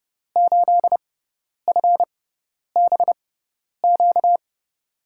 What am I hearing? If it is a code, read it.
Morse code, heard as a keyed tone, about 30 wpm, spaced as for 10 wpm, 700 Hz. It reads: 8FBQ
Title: USA Call Signs - 6 Characters - 6x Character Spacing - 30wpm